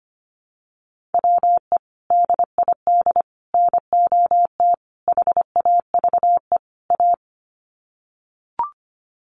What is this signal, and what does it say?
Morse code, heard as a keyed tone, about 25 wpm, 700 Hz. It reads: WE DID NOT HAVE A
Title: Sets of 5 Words Encore2 25wpm